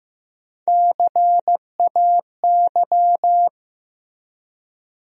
Morse code, heard as a keyed tone, about 15 wpm, 700 Hz. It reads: CAY
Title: USA Call Sign Postfixes - 3 Letters 15wpm